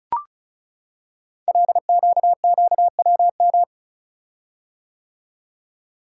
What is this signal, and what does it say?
Morse code, heard as a keyed tone, about 36 wpm, 690 Hz. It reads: LQQWM
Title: Code Groups - 5 Letters - 35wpm